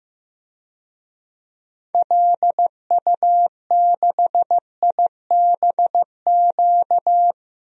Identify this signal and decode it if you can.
Morse code, heard as a keyed tone, about 15 wpm, 700 Hz. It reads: LU6IBQ
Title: DXCC-100 Call Signs - 6 Characters - Repeated 2x - 15wpm